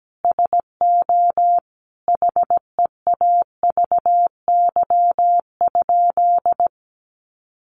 Morse code, heard as a keyed tone, about 17 wpm, 700 Hz. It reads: SO HEAVY?
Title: Sentences from Top 1100 Words 17wpm